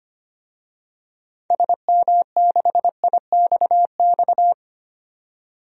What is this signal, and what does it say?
Morse code, heard as a keyed tone, about 25 wpm, 700 Hz. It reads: SM6IXX
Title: DXCC-100 Call Signs - 6 Characters - 25wpm